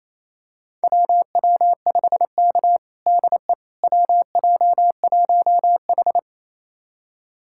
Morse code, heard as a keyed tone, about 28 wpm, 700 Hz. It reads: WW5K DE WJ1H
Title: USA Callsign DE Callsign - 4 Characters - 28wpm